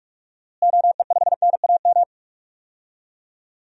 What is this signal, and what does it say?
Morse code, heard as a keyed tone, about 46 wpm, 700 Hz. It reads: OE5NAM
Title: DXCC-100 Call Signs - 6 Characters - Repeated 2x - 45wpm